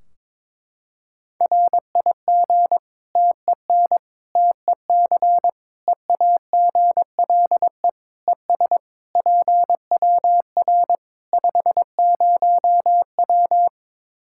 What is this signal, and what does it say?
Morse code, heard as a keyed tone, about 22 wpm, 700 Hz. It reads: RIG TEN TEC EAGLE ES PWR 50W
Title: Practice QSO - Exchange 2 - Info - 22wpm